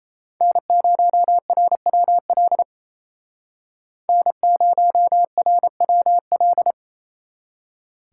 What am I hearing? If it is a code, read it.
Morse code, heard as a keyed tone, about 37 wpm, 700 Hz. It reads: N0RWL N0RWL